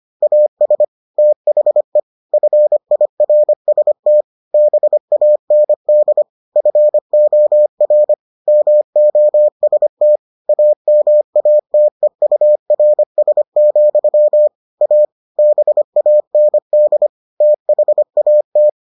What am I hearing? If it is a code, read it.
Morse code, heard as a keyed tone, about 25 wpm, 600 Hz. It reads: AS THE FIRST BAND FOR MOST AMATEURS, A BAND THAT